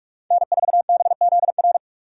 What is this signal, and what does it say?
Morse code, heard as a keyed tone, about 48 wpm, 700 Hz. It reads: N4BZR
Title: USA Call Signs - 5 Characters - 3x Character Spacing - 45wpm